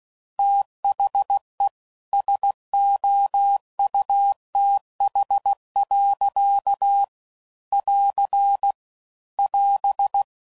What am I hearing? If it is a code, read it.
Morse code, heard as a keyed tone, about 16 wpm, 800 Hz. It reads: THE SOUTH. <AR> <AS>
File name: CW Code Groups @15WPM / Weather Sets / Weather 4 10